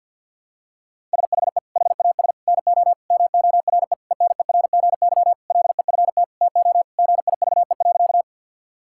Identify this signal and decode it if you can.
Morse code, heard as a keyed tone, about 47 wpm, 700 Hz. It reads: SHE HAS NO MORE ENERGY LEFT TO GIVE.
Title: Sentences from Top 1000 Words - Review Part 6 - 50wpm